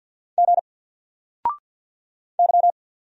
Morse code, heard as a keyed tone, about 48 wpm, 700 Hz. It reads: G Y
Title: Single Character (Letter:Number:Punctuation:Prosign) 50wpm